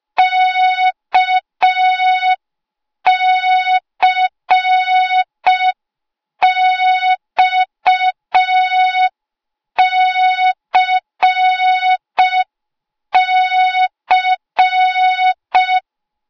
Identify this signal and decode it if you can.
Morse code, heard as a keyed tone, about 5 wpm, 3700 Hz. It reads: KCXCC
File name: Morze-HA2MN / hang / 08KXC HA2MN